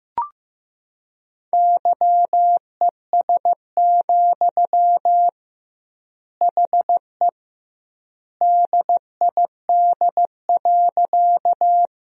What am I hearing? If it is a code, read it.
Morse code, heard as a keyed tone, about 15 wpm, 700 Hz. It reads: YES, HE DID.
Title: 2x Word-Spacing - Sentences from Top 500 Words - 15wpm